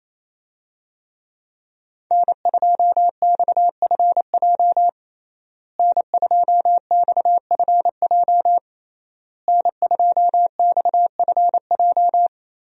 Morse code, heard as a keyed tone, about 28 wpm, 700 Hz. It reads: N2XFJ N2XFJ N2XFJ
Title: USA Call Signs - 5 Characters - Repeated 3x - 28wpm